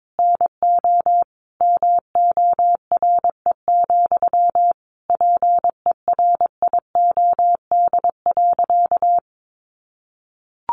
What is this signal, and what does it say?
Morse code, heard as a keyed tone, about 22 wpm, 700 Hz. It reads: NO MORE, PERIOD.
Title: Sentences from Top 800 Words 22wpm